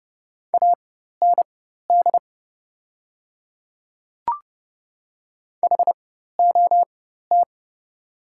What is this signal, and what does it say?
Morse code, heard as a keyed tone, about 30 wpm, spaced as for 14 wpm, 700 Hz. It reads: AND HOT
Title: 4x Character Spacing - Top 500 Words - 30wpm